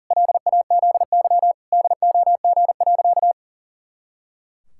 Morse code, heard as a keyed tone, about 40 wpm, 700 Hz. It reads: LAZY DOG.
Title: Pangram Challenge v2 - 40wpm